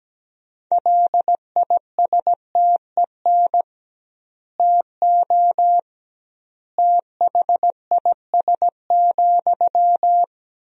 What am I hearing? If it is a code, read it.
Morse code, heard as a keyed tone, about 17 wpm, 700 Hz. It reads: LISTEN TO THIS,
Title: 2x Word-Spacing - Sentences from Top 400 Words - 17wpm